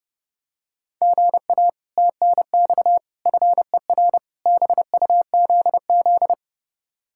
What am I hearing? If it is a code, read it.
Morse code, heard as a keyed tone, about 29 wpm, 700 Hz. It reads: GA TNX FER BUZZ